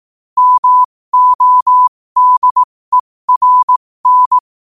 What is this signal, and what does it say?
Morse code, heard as a keyed tone, about 18 wpm, spaced as for 15 wpm, 1000 Hz. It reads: MODERN